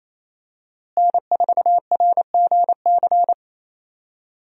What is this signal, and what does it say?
Morse code, heard as a keyed tone, about 28 wpm, 700 Hz. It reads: N4RGC
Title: USA Call Signs - 5 Characters - Repeated 3x - 28wpm